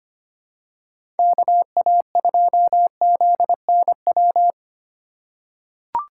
Morse code, heard as a keyed tone, about 25 wpm, 700 Hz. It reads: KA2ZNW
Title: USA Call Signs - 6 Characters - 4x Character Spacing - 25wpm